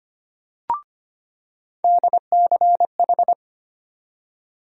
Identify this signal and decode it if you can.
Morse code, heard as a keyed tone, about 25 wpm, 700 Hz. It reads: DCH
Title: USA Call Sign Postfixes - 3 Letters 25wpm